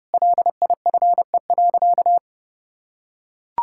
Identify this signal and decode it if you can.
Morse code, heard as a keyed tone, about 30 wpm, 700 Hz. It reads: LIFE.
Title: Sentences from Top 1000 Words - Review Part 5 - 30wpm